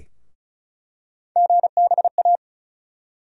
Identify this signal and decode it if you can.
Morse code, heard as a keyed tone, about 36 wpm, 700 Hz. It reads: GBA
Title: USA Call Sign Postfixes - 3 Letters 35wpm